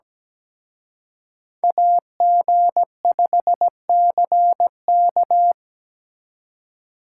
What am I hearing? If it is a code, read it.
Morse code, heard as a keyed tone, about 20 wpm, 700 Hz. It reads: AG5CK
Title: Speed Racing - USA 2x2 Call Signs 17wpm